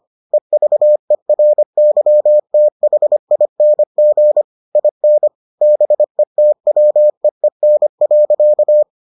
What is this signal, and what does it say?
Morse code, heard as a keyed tone, about 25 wpm, 600 Hz. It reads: EVERYTHING IN BETWEEN.